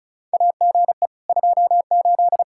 Morse code, heard as a keyed tone, about 36 wpm, 700 Hz. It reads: AGE 28
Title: Practice QSO - Exchange 3 - More Info - 35wpm